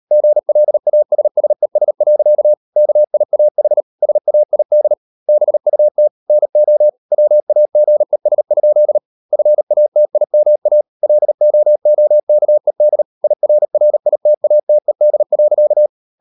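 Morse code, heard as a keyed tone, about 38 wpm, 600 Hz. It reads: GLASSES. KIAH SAID BUT NO WAGES? FATIMA LOOKED IRRITATED.
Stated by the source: Morse code practice at 38 WPM, sending Ken Follett's novel Never in English, from CW med Gnister.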